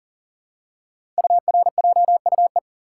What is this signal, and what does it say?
Morse code, heard as a keyed tone, about 38 wpm, 700 Hz. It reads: URJUE
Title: Code Groups - 5 Letters - 40wpm